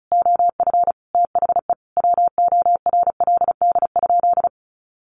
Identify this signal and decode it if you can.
Morse code, heard as a keyed tone, about 34 wpm, 700 Hz. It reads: OF THE WORLD?
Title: Sentences from Top 200 Words 35wpm